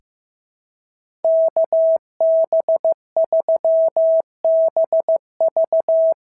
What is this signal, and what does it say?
Morse code, heard as a keyed tone, about 16 wpm, 690 Hz. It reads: KB3BV
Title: SOTA QSO Practice 15wpm